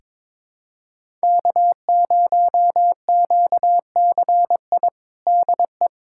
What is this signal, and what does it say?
Morse code, heard as a keyed tone, about 21 wpm, 700 Hz. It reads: K0QCI DE
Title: USA Callsign DE Callsign - 5 Characters - 22wpm